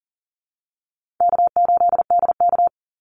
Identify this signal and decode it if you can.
Morse code, heard as a keyed tone, about 40 wpm, 700 Hz. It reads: K8DK
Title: Easy Call Signs - 40wpm